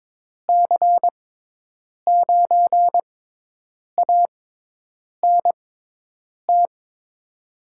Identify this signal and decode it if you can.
Morse code, heard as a keyed tone, about 22 wpm, spaced as for 8 wpm, 700 Hz. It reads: C9ANT
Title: USA Call Signs - 6x Character Spacing - 22wpm